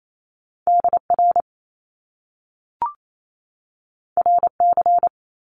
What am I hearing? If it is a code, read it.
Morse code, heard as a keyed tone, about 28 wpm, 700 Hz. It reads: DR RC
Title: Top Two Letter Combinations in Words 28wpm